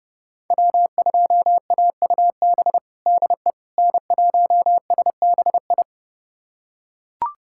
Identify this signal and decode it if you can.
Morse code, heard as a keyed tone, about 32 wpm, 700 Hz. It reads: W2AUB DE N1SBI
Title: USA Callsign DE Callsign 30wpm